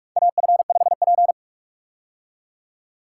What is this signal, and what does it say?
Morse code, heard as a keyed tone, about 45 wpm, 700 Hz. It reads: AF5P
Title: K1USN SST QSO Practice - 45wpm